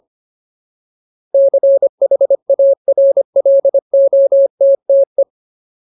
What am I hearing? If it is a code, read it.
Morse code, heard as a keyed tone, about 25 wpm, 550 Hz. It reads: CHARLOTTE
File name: Ultimate Word List / 25WPM Ultimate